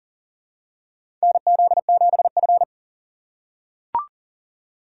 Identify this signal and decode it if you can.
Morse code, heard as a keyed tone, about 37 wpm, 700 Hz. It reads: NZ7F
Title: USA Call Signs - 4 Characters - 3x Character Spacing - 40wpm